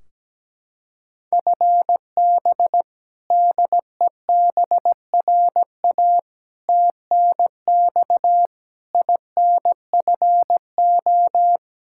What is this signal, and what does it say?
Morse code, heard as a keyed tone, about 17 wpm, 700 Hz. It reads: FB DEBRA TNX INFO
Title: Practice QSO - Exchange 3 - More Info - 17wpm